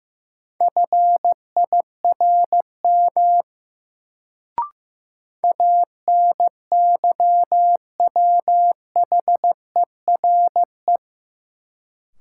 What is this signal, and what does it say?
Morse code, heard as a keyed tone, about 15 wpm, 700 Hz. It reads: FIRM ANYWHERE
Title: Sentences from Top 1400 Words - New Word Drill Alt - 15wpm